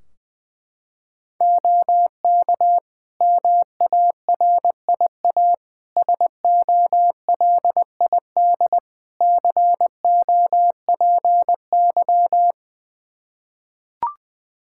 Morse code, heard as a keyed tone, about 20 wpm, 700 Hz. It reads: OK MARIA SOLID COPY
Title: Practice QSO - Exchange 3 - More Info - 20wpm